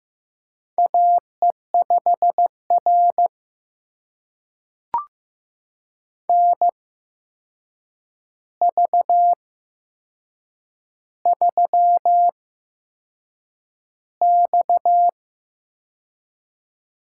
Morse code, heard as a keyed tone, about 15 wpm, spaced as for 6 wpm, 700 Hz. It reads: AE5R NV3X
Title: USA Call Signs - 4 Characters - 8x Character Spacing - 15wpm